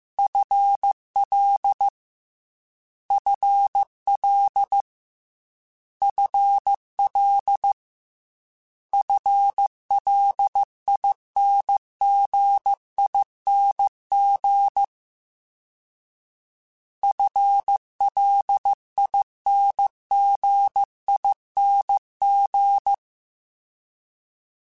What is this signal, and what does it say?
Morse code, heard as a keyed tone, about 15 wpm, 800 Hz. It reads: FL FL FL FLINGING FLINGING
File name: FL / FL mp3s / FL Prefixes 3x2 Twice @13